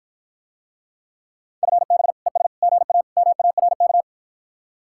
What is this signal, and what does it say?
Morse code, heard as a keyed tone, about 54 wpm, 700 Hz. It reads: FB ES GA MARK